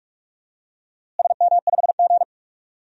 Morse code, heard as a keyed tone, about 46 wpm, 700 Hz. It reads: SM5G
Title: DXCC-100 Call Signs - 4 Characters - Repeated 2x - 45wpm